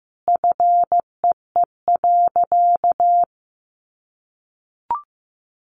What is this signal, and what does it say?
Morse code, heard as a keyed tone, about 15 wpm, 700 Hz. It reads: FEE.
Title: Sentences from Top 1500 Words 15wpm